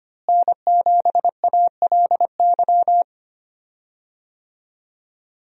Morse code, heard as a keyed tone, about 25 wpm, 700 Hz. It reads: N7ALY
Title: USA Call Signs - 5 Characters - Repeated 2x - 25wpm